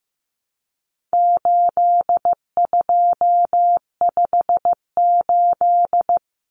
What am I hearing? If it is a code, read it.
Morse code, heard as a keyed tone, about 15 wpm, 700 Hz. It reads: 8258
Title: Common Cut Numbers - 4-Digit 15wpm